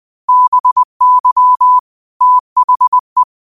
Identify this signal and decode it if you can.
Morse code, heard as a keyed tone, about 20 wpm, 1000 Hz. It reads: BY THE
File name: qotdfiles / 20wpm